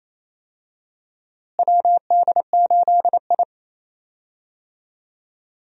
Morse code, heard as a keyed tone, about 28 wpm, 700 Hz. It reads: WD8I
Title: USA Call Signs - 4 Characters - Repeated 2x - 28wpm